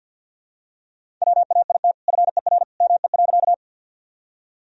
Morse code, heard as a keyed tone, about 52 wpm, 700 Hz. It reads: WAIT FER ME.
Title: Sentences from Top 500 Words and Abbreviations 50wpm